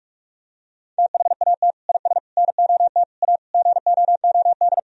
Morse code, heard as a keyed tone, about 43 wpm, 700 Hz. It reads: THAT IS NOT A GOOD
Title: Sentences from Top 1000 Words - Review Part 3 - 45wpm